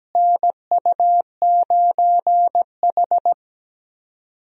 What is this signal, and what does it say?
Morse code, heard as a keyed tone, about 17 wpm, 700 Hz. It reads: NU9H
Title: USA Call Signs - 4 Characters - Repeated 3x - 17wpm